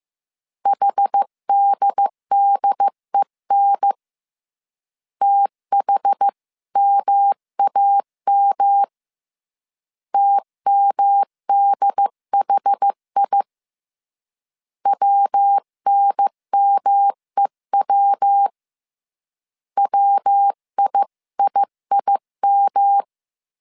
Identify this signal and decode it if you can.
Morse code, heard as a keyed tone, about 15 wpm, 800 Hz. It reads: HDDEN THMAM TMDHI WNMEW WIIIM
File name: CW / DET / 1 3 1500C S C 14wpm 2458